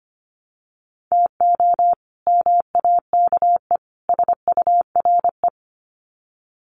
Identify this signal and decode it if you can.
Morse code, heard as a keyed tone, about 25 wpm, 700 Hz. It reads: TO MAKE SURE
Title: Sets of 3 Words 25wpm